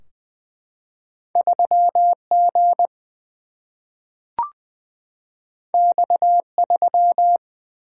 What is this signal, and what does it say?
Morse code, heard as a keyed tone, about 20 wpm, 700 Hz. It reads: 3 G X 3
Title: Single-Letter Single-Number 20wpm